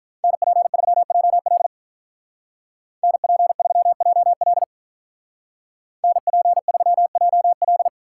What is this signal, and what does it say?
Morse code, heard as a keyed tone, about 43 wpm, 700 Hz. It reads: NP3JL NP3JL NP3JL
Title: Speed Racing - USA 2x2 Call Signs 35wpm